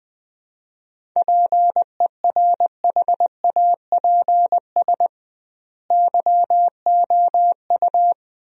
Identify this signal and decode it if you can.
Morse code, heard as a keyed tone, about 20 wpm, 700 Hz. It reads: PERHAPS YOU
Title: 2x Word-Spacing - Sentences from Top 500 Words - 20wpm